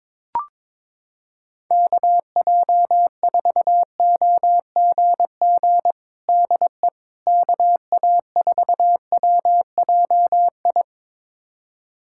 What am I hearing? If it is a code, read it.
Morse code, heard as a keyed tone, about 22 wpm, 700 Hz. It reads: KJ4OGG DE KA4WJI
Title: USA Callsign DE Callsign - 6 Characters - 22wpm